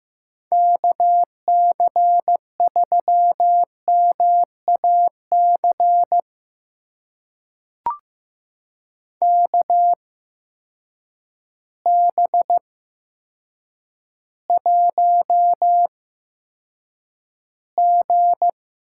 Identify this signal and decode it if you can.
Morse code, heard as a keyed tone, about 15 wpm, spaced as for 8 wpm, 700 Hz. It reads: KC3MAC KB1G
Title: USA Call Signs - 8x Character Spacing - 15wpm